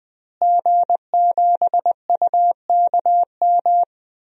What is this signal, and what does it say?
Morse code, heard as a keyed tone, about 20 wpm, 700 Hz. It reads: G7UKM
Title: DXCC-100 Call Signs - 5 Characters - Repeated 2x - 20wpm